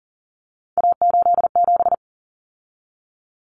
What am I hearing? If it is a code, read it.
Morse code, heard as a keyed tone, about 40 wpm, 700 Hz. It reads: A87
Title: Common Cut Numbers - 3-Digit 40wpm